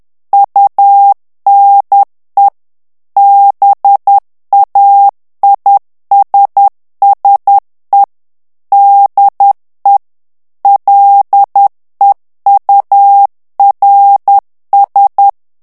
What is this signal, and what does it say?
Morse code, heard as a keyed tone, about 10 wpm, 800 Hz. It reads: UNE BAISSE DE LEURS
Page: Français - texte en clair 1 - vitesse : 10 mots minute